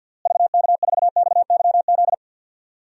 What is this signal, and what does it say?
Morse code, heard as a keyed tone, about 53 wpm, 700 Hz. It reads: VK4XYZ
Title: DXCC-100 Call Signs - 6 Characters - Repeated 3x - 50wpm